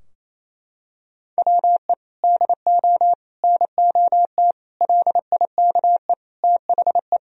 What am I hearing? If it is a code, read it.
Morse code, heard as a keyed tone, about 28 wpm, 700 Hz. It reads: WE DO NOT LIKE THE
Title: Sentences from Top 1000 Words - Review Part 4 - 28wpm